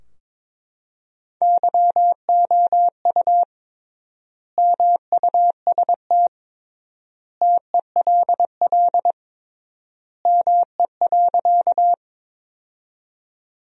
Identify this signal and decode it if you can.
Morse code, heard as a keyed tone, about 22 wpm, 700 Hz. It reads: YOU MUST TELL ME.
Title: 3x Word-Spacing - Sentences from Top 200 Words - 22wpm